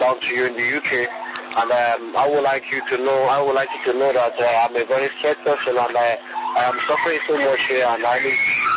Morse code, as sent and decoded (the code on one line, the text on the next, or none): none